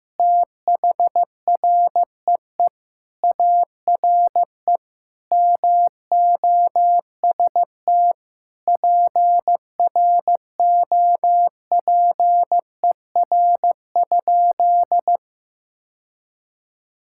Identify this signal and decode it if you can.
Morse code, heard as a keyed tone, about 15 wpm, 700 Hz. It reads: THREE ARE MOST PROPER?
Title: Sentences from Top 1000 Words - Review Part 5 - 15wpm